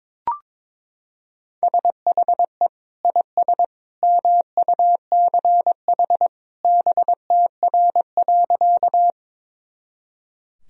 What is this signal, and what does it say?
Morse code, heard as a keyed tone, about 22 wpm, 700 Hz. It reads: SHE IS MUCH BTR.
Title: Sentences from Top 400 Words and Abbreviations 22wpm